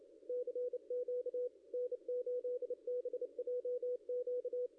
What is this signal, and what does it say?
Morse code, heard as a keyed tone, about 31 wpm, 490 Hz, with radio noise. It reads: CQ N8BJQ
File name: contests / ss-cw-k5tr-op-n5rz -0700